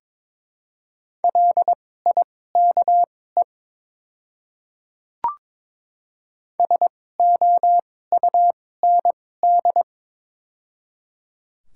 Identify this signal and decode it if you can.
Morse code, heard as a keyed tone, about 22 wpm, spaced as for 16 wpm, 700 Hz. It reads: LIKE SOUND
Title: 2x Character Spacing - Top 100 Words - 22wpm